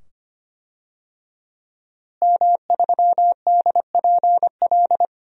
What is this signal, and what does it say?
Morse code, heard as a keyed tone, about 25 wpm, 700 Hz. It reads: M3DPL
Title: DXCC-100 Call Signs - 5 Characters - Repeated 2x - 25wpm